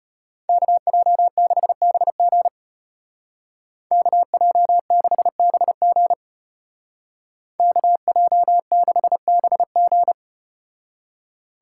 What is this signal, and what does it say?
Morse code, heard as a keyed tone, about 34 wpm, 700 Hz. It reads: KJ6BG KJ6BG KJ6BG